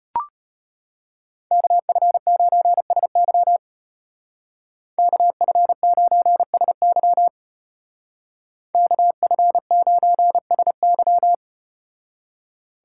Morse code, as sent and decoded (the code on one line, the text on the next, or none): -.- ..-. ----. ... -.-- / -.- ..-. ----. ... -.-- / -.- ..-. ----. ... -.--
KF9SY KF9SY KF9SY